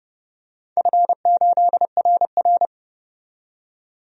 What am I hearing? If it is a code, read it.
Morse code, heard as a keyed tone, about 30 wpm, 700 Hz. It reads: F8RR
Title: DXCC-100 Call Signs - 4 Characters - 30wpm